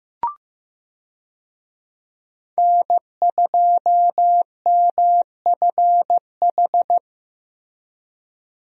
Morse code, heard as a keyed tone, about 15 wpm, 700 Hz. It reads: N2MFH
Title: USA Call Signs - 5 Characters - Repeated 2x - 15wpm